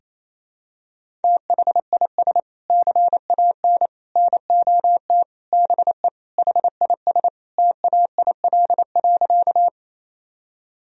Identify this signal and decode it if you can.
Morse code, heard as a keyed tone, about 27 wpm, 700 Hz. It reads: THIS CAN NOT BE HIS TAIL.
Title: Sentences from Top 1000 Words - Review Part 3 - 28wpm